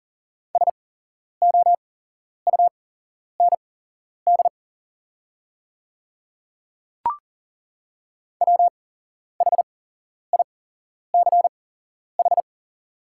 Morse code, heard as a keyed tone, about 40 wpm, spaced as for 11 wpm, 700 Hz. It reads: SOUND WHICH
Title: 8x Character Spacing - Top 200 Words - 40wpm